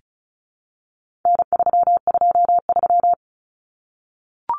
Morse code, heard as a keyed tone, about 34 wpm, 700 Hz. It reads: N323
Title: Common Cut Numbers - 4-Digit 35wpm